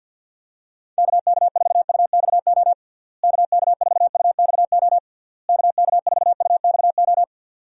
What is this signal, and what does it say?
Morse code, heard as a keyed tone, about 48 wpm, 700 Hz. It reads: KK4UXO KK4UXO KK4UXO
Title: USA Call Signs - 6 Characters - Repeated 3x - 50wpm